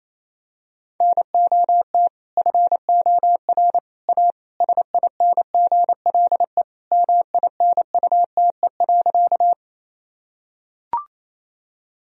Morse code, heard as a keyed tone, about 28 wpm, 700 Hz. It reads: NOT FOR A SINGLE MINUTE.
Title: Sentences from Top 1000 Words - Review Part 4 - 28wpm